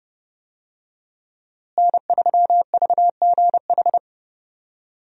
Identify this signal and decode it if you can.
Morse code, heard as a keyed tone, about 30 wpm, 700 Hz. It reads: N3VGH
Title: USA Call Signs - 5 Characters - Repeated 3x - 30wpm